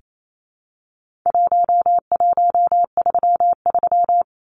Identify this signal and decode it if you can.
Morse code, heard as a keyed tone, about 28 wpm, 700 Hz. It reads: 1133
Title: Numbers - 4-Digit 28wpm